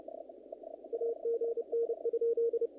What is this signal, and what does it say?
Morse code, heard as a keyed tone, about 32 wpm, 460 Hz. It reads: AGN?
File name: contests / naqp-cw-k5tr-op-k5pi- -2000